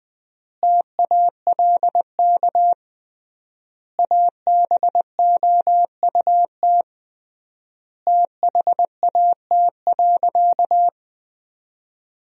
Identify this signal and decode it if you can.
Morse code, heard as a keyed tone, about 20 wpm, 700 Hz. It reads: TALK ABOUT THAT.
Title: 3x Word-Spacing - Sentences from Top 400 Words - 20wpm